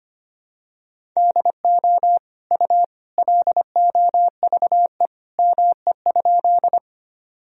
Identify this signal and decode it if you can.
Morse code, heard as a keyed tone, about 25 wpm, 700 Hz. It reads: DO U LOVE ME?
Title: Sentences from Top 400 Words and Abbreviations 25wpm